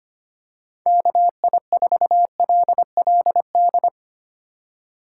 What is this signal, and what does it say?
Morse code, heard as a keyed tone, about 25 wpm, 700 Hz. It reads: KI4LLD
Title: USA Call Signs - 6 Characters - 3x Character Spacing - 25wpm